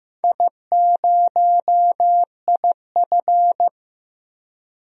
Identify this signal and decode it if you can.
Morse code, heard as a keyed tone, about 15 wpm, 700 Hz. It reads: I0IF
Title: DXCC-100 Call Signs - 4 Characters - 15wpm